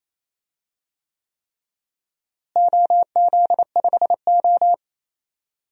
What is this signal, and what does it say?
Morse code, heard as a keyed tone, about 27 wpm, 700 Hz. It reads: OZ5O
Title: DXCC-100 Call Signs - 4 Characters - Repeated 3x - 28wpm